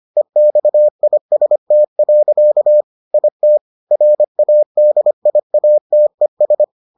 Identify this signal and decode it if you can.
Morse code, heard as a keyed tone, about 25 wpm, 600 Hz. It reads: EXIST. IT RADIATES